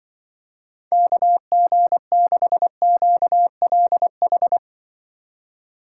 Morse code, heard as a keyed tone, about 25 wpm, 700 Hz. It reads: KG6QLH
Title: Speed Racing - USA Call Signs 20wpm